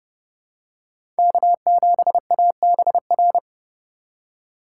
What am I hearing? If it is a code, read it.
Morse code, heard as a keyed tone, about 31 wpm, 700 Hz. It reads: K7ABR
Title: USA Call Signs - 5 Characters - 2x Character Spacing - 30wpm